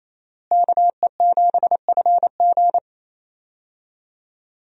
Morse code, heard as a keyed tone, about 28 wpm, 700 Hz. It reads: KE7FG
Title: USA Call Signs - 5 Characters - Repeated 2x - 28wpm